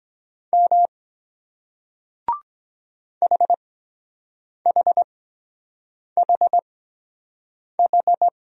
Morse code, heard as a keyed tone, about 23 wpm, spaced as for 7 wpm, 700 Hz. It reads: M HHHH